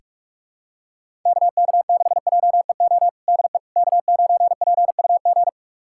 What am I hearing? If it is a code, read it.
Morse code, heard as a keyed tone, about 47 wpm, 700 Hz. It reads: KK6JEO DE K9PUG
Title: USA Callsign DE Callsign 45wpm